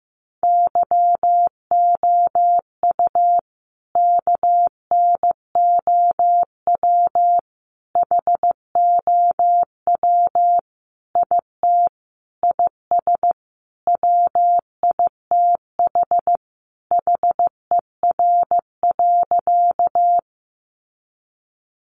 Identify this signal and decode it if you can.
Morse code, heard as a keyed tone, about 15 wpm, 700 Hz. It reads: YOU KNOW HOW IT IS WITH HER.
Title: Sentences from Top 100 Words 15wpm